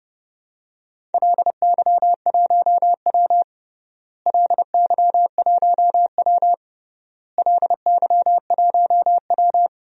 Morse code, heard as a keyed tone, about 32 wpm, 700 Hz. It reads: LY1W LY1W LY1W
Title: DXCC-100 Call Signs - 4 Characters - Repeated 3x - 30wpm